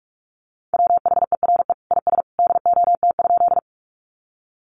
Morse code, heard as a keyed tone, about 44 wpm, 700 Hz. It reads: WHERE IS DOT?
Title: Sentences from Top 1500 Words 45wpm